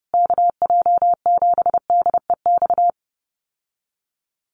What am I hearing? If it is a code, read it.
Morse code, heard as a keyed tone, about 28 wpm, 700 Hz. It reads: KJ7DEX
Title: USA Call Signs Repeated 3x 30wpm